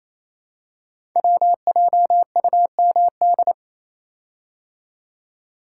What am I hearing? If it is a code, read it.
Morse code, heard as a keyed tone, about 28 wpm, 700 Hz. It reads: WJUMD
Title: Code Groups - 5 Letters - 28wpm